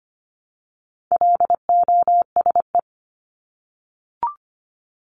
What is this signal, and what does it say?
Morse code, heard as a keyed tone, about 25 wpm, 700 Hz. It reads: LOSE
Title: Top 1001-1250 Words 25wpm